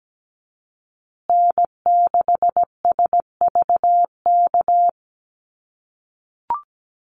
Morse code, heard as a keyed tone, about 17 wpm, 700 Hz. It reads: N6SVK
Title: USA Call Signs Repeated 2x 17wpm